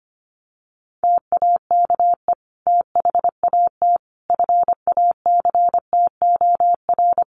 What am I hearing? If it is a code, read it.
Morse code, heard as a keyed tone, about 25 wpm, 700 Hz. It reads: TAKE THAT FACTOR
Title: Sentences from Top 1300 Words 25wpm